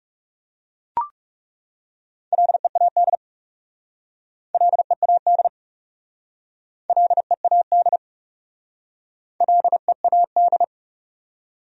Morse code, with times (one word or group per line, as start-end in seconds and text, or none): 2.3-3.1 LEAD
4.5-5.5 LEAD
6.9-8.0 LEAD
9.4-10.6 LEAD